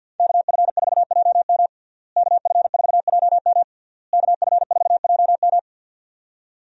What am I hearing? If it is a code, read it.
Morse code, heard as a keyed tone, about 52 wpm, 700 Hz. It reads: KF4JM KF4JM KF4JM
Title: USA Call Signs - 5 Characters - Repeated 3x - 50wpm